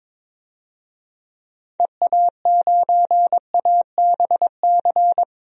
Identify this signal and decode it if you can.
Morse code, heard as a keyed tone, about 22 wpm, 700 Hz. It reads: EA9ABC
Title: DXCC-100 Call Signs - 6 Characters - 22wpm